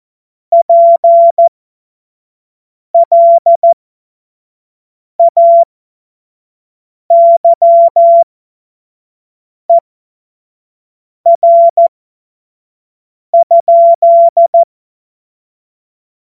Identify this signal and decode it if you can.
Morse code, heard as a keyed tone, about 14 wpm, spaced as for 5 wpm, 680 Hz. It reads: PLAYER?
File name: Morsekurs / Witze / Witz 12. Oktober 2021, 25 F70